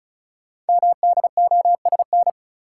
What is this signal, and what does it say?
Morse code, heard as a keyed tone, about 36 wpm, 690 Hz. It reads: MDOSN
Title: Code Groups - 5 Letter:Numbers - 35wpm